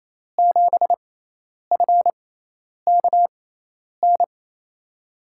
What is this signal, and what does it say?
Morse code, heard as a keyed tone, about 28 wpm, spaced as for 10 wpm, 700 Hz. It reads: 7FKN
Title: USA Call Signs - 5 Characters - 6x Character Spacing - 28wpm